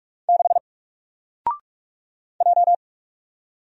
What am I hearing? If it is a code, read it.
Morse code, heard as a keyed tone, about 44 wpm, 700 Hz. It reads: 6 J
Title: Single Letter-Number Alt 45wpm